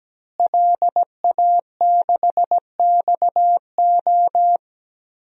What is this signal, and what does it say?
Morse code, heard as a keyed tone, about 17 wpm, 700 Hz. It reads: LA6XO
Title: DXCC-100 Call Signs - 5 Characters - Repeated 3x - 17wpm